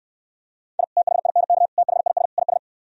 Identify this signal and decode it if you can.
Morse code, heard as a keyed tone, about 68 wpm, 700 Hz. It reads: I THINK THIS IS